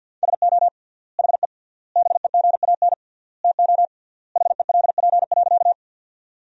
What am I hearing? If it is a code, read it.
Morse code, heard as a keyed tone, about 47 wpm, 700 Hz. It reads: SO HE BEGAN TO HELP.
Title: 3x Word-Spacing - Sentences from Top 300 Words - 50wpm